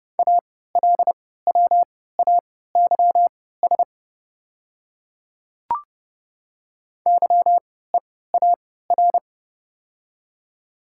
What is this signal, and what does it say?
Morse code, heard as a keyed tone, about 30 wpm, spaced as for 17 wpm, 690 Hz. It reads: ALWAYS YEAR
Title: 3x Character Spacing - Top 300 Words - 30wpm